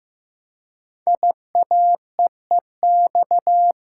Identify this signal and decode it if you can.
Morse code, heard as a keyed tone, about 15 wpm, 700 Hz. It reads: IAEEX
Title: Code Groups - 5 Letter:Numbers - 15wpm